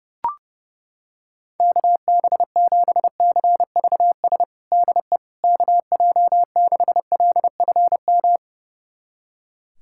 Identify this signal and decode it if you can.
Morse code, heard as a keyed tone, about 32 wpm, 700 Hz. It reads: KB7CVS DE KJ6LFM
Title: USA Callsign DE Callsign - 6 Characters - 30wpm